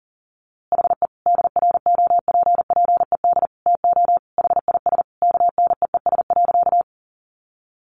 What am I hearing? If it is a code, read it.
Morse code, heard as a keyed tone, about 40 wpm, 700 Hz. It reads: HE DROPPED TO HIS KNEES.
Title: A Sound Of Thunder 40wpm